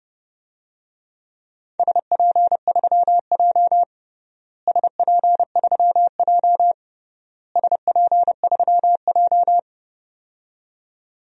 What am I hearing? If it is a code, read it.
Morse code, heard as a keyed tone, about 30 wpm, 700 Hz. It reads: SP3J SP3J SP3J
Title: DXCC-100 Call Signs - 4 Characters - Repeated 3x - 30wpm